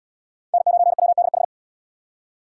Morse code, heard as a keyed tone, about 75 wpm, 700 Hz. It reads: N9WDH